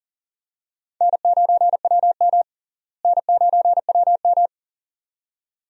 Morse code, heard as a keyed tone, about 40 wpm, 700 Hz. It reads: N9WM N9WM
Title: USA Call Signs - 4 Characters - Repeated 2x - 40wpm